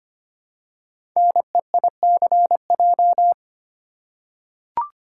Morse code, heard as a keyed tone, about 26 wpm, 700 Hz. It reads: NEICJ
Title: Code Groups - 5 Letters - 25wpm